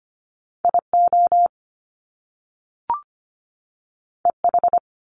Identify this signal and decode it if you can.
Morse code, heard as a keyed tone, about 25 wpm, 700 Hz. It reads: I O E H
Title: Top Two Letter Combinations in Words 25wpm